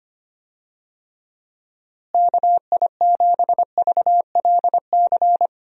Morse code, heard as a keyed tone, about 25 wpm, 700 Hz. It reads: KI7VLC
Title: USA Call Signs - 6 Characters - Repeated 3x - 25wpm